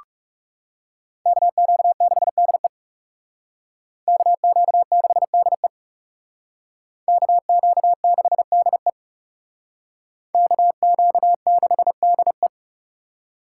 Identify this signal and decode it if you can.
Morse code, heard as a keyed tone, about 38 wpm, 700 Hz. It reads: KQ6DE KQ6DE KQ6DE KQ6DE